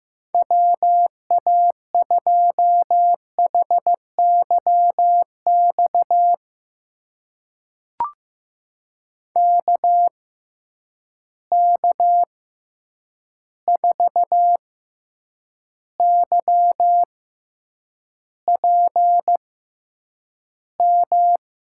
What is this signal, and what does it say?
Morse code, heard as a keyed tone, about 15 wpm, spaced as for 8 wpm, 700 Hz. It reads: WA2HYX KK4YPM
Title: USA Call Signs - 6 Characters - 6x Character Spacing - 15wpm